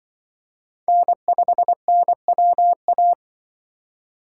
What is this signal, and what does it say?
Morse code, heard as a keyed tone, about 25 wpm, 700 Hz. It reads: N5NWA
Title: Speed Racing - USA Call Signs 20wpm